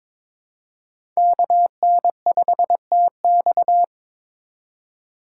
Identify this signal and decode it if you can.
Morse code, heard as a keyed tone, about 22 wpm, 700 Hz. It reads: KN5TX
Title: K1USN SST QSO Practice - 22wpm